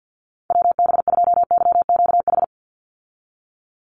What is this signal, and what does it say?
Morse code, heard as a keyed tone, about 50 wpm, 700 Hz. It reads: WB2YQH
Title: Call Signs Repeated 2x - 50wpm